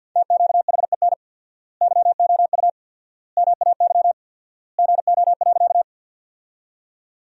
Morse code, heard as a keyed tone, about 50 wpm, 700 Hz. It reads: TQHEN YOU MAY GO.
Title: 4x Word-Spacing - Sentences from Top 100 Words - 50wpm